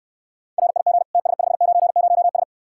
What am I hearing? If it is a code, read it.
Morse code, heard as a keyed tone, about 69 wpm, 700 Hz. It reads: RIG TS590S